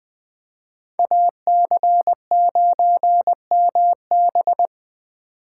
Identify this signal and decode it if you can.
Morse code, heard as a keyed tone, about 23 wpm, 700 Hz. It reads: AC9MB
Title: Speed Racing - USA Call Signs 20wpm